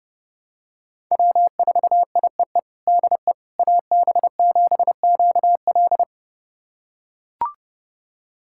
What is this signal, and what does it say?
Morse code, heard as a keyed tone, about 30 wpm, 700 Hz. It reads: W4IEE DE AB7QL
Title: USA Callsign DE Callsign - 5 Characters - 30wpm